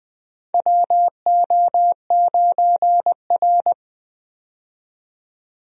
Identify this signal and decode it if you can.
Morse code, heard as a keyed tone, about 20 wpm, 700 Hz. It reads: WO9R
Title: USA Call Signs - 4 Characters - 20wpm